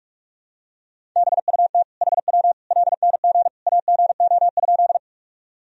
Morse code, heard as a keyed tone, about 42 wpm, 700 Hz. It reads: BUT HW LNG AGO?
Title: Sentences from Top 500 Words and Abbreviations 45wpm